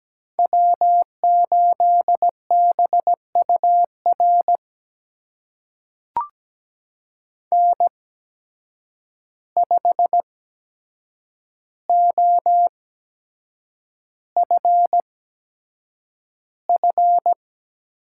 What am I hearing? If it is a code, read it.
Morse code, heard as a keyed tone, about 17 wpm, spaced as for 7 wpm, 700 Hz. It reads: W8BUR N5OFF
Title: USA Call Signs - 5 Characters - 8x Character Spacing - 17wpm